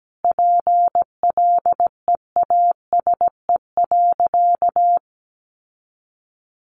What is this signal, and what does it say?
Morse code, heard as a keyed tone, about 17 wpm, 700 Hz. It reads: PLEASE.
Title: Sentences from Top 1500 Words 17wpm